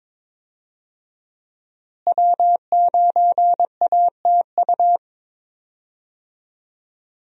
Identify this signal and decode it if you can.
Morse code, heard as a keyed tone, about 22 wpm, 700 Hz. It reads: W9ATU
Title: USA Call Signs - 5 Characters - Repeated 3x - 22wpm